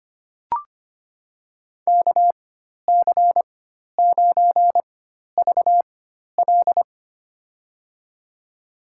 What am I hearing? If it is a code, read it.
Morse code, heard as a keyed tone, about 25 wpm, spaced as for 12 wpm, 700 Hz. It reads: KC9VL
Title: USA Call Signs - 5 Characters - 4x Character Spacing - 25wpm